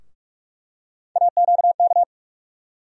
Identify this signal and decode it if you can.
Morse code, heard as a keyed tone, about 42 wpm, 700 Hz. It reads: AQK
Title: USA Call Sign Postfixes - 3 Letters 45wpm